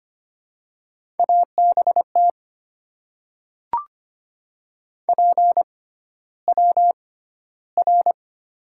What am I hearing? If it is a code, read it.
Morse code, heard as a keyed tone, about 25 wpm, spaced as for 13 wpm, 700 Hz. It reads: ABT PWR
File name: Top 90 QSO Elements - 6x Character Spacing - 25wpm